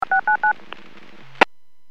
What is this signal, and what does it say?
DTMF, heard as telephone touch tones, tones of about 110 ms. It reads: 699
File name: WPS / 00852 21-25 WPS 451.25 WPS